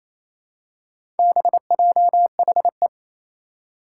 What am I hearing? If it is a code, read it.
Morse code, heard as a keyed tone, about 28 wpm, 700 Hz. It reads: BJHE
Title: Four Challenging Letter-Number Combinations 28wpm